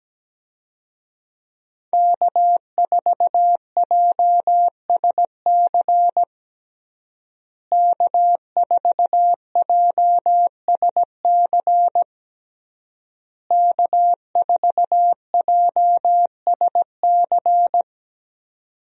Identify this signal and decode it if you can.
Morse code, heard as a keyed tone, about 17 wpm, 700 Hz. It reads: K4JSC K4JSC K4JSC
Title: USA Call Signs - 5 Characters - Repeated 3x - 17wpm